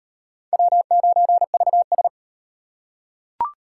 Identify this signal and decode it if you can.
Morse code, heard as a keyed tone, about 38 wpm, 700 Hz. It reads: W9VS